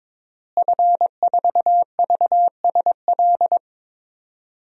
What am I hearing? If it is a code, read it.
Morse code, heard as a keyed tone, about 22 wpm, 700 Hz. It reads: F4VSL
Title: DXCC-100 Call Signs - 5 Characters - Repeated 2x - 22wpm